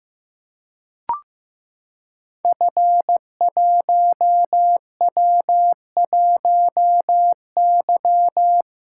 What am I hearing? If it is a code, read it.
Morse code, heard as a keyed tone, about 15 wpm, 700 Hz. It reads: F1W1Y
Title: Code Groups - 5 Letter:Numbers - 15wpm